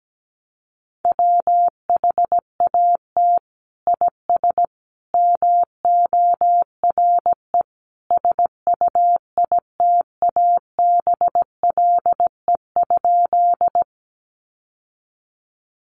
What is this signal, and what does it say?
Morse code, heard as a keyed tone, about 17 wpm, 700 Hz. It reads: WHAT IS MORE SUITABLE?
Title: Sentences from Top 1400 Words 17wpm